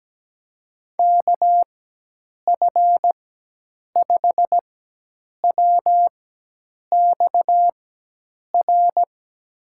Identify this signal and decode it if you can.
Morse code, heard as a keyed tone, about 17 wpm, spaced as for 8 wpm, 700 Hz. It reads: KF5WXR
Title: USA Call Signs - 6 Characters - 4x Character Spacing - 17wpm